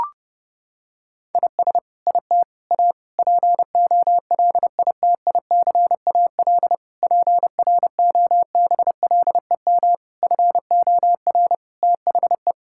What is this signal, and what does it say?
Morse code, heard as a keyed tone, about 30 wpm, 700 Hz. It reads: IS IT A POLITICAL PROBLEM FOR THE
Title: Sentences from Top 1000 Words - Review Part 5 - 30wpm